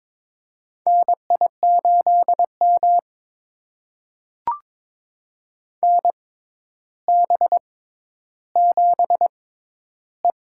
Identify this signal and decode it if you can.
Morse code, heard as a keyed tone, about 23 wpm, spaced as for 11 wpm, 700 Hz. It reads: NI8M NB7E
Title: USA Call Signs - 4 Characters - 6x Character Spacing - 22wpm